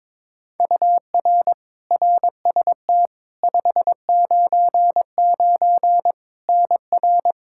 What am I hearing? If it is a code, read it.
Morse code, heard as a keyed tone, about 22 wpm, 700 Hz. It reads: UR RST 599 NR
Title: Practice QSO - Exchange 1 - Beginning - 22wpm